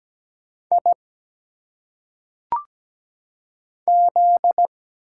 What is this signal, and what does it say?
Morse code, heard as a keyed tone, about 17 wpm, 700 Hz. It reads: I Z
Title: Single Letters - Phonetic Pronunciation 17wpm